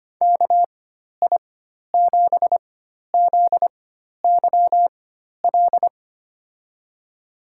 Morse code, heard as a keyed tone, about 25 wpm, spaced as for 12 wpm, 700 Hz. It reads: KI7ZYL
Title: USA Call Signs - 6 Characters - 4x Character Spacing - 25wpm